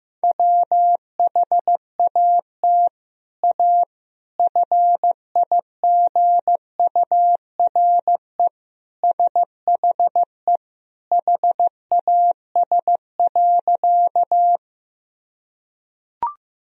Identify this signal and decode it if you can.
Morse code, heard as a keyed tone, about 15 wpm, 700 Hz. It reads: WHAT A FIGURE SHE HAS.
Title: Sentences from Top 1000 Words - Review Part 2 - 15wpm